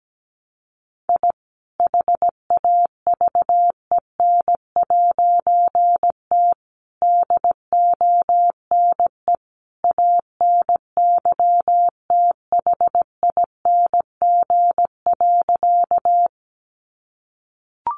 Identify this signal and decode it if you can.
Morse code, heard as a keyed tone, about 17 wpm, 700 Hz. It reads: I HAVEN'T DONE ANYTHING.
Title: A Sound Of Thunder 17wpm